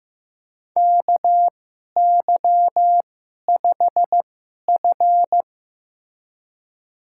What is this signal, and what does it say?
Morse code, heard as a keyed tone, about 15 wpm, spaced as for 11 wpm, 700 Hz. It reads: KY5F
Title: USA Call Signs - 4 Characters - 2x Character Spacing - 15wpm